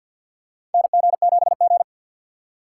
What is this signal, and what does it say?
Morse code, heard as a keyed tone, about 51 wpm, 700 Hz. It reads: NG7G
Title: USA Call Signs - 4 Characters - 2x Character Spacing - 50wpm